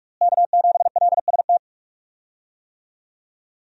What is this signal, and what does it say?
Morse code, heard as a keyed tone, about 46 wpm, 700 Hz. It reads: K7LST
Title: USA Call Signs - 5 Characters - Repeated 2x - 45wpm